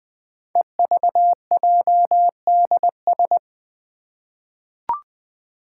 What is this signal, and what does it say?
Morse code, heard as a keyed tone, about 20 wpm, 700 Hz. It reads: EVJDS
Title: Code Groups - 5 Letters - 20wpm